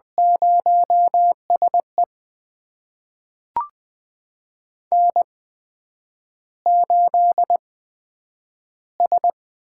Morse code, heard as a keyed tone, about 20 wpm, spaced as for 9 wpm, 700 Hz. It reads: 0SE N8S
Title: USA Call Signs - 8x Character Spacing - 20wpm